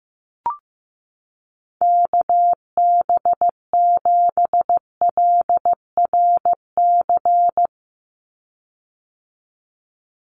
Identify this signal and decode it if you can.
Morse code, heard as a keyed tone, about 15 wpm, 700 Hz. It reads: KB7LRC
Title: USA Call Signs 15wpm